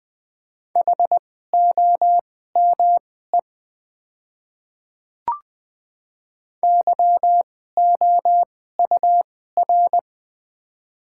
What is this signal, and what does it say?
Morse code, heard as a keyed tone, about 20 wpm, spaced as for 14 wpm, 700 Hz. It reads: HOME YOUR
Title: 2x Character Spacing - Top 200 Words - 20wpm